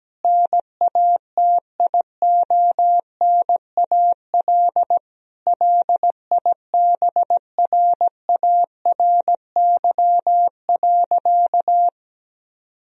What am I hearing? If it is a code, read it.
Morse code, heard as a keyed tone, about 17 wpm, 700 Hz. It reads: NATIONAL LIBRARY.
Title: Sentences from Top 1000 Words - Review Part 4 - 17wpm